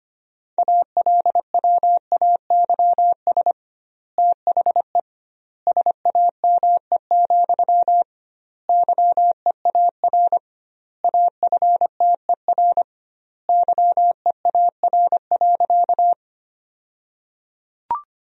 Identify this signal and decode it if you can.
Morse code, heard as a keyed tone, about 25 wpm, 700 Hz. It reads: ALWAYS THE SAME, YEAR AFTER YEAR.
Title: 2x Word-Spacing - Sentences from Top 300 Words - 25wpm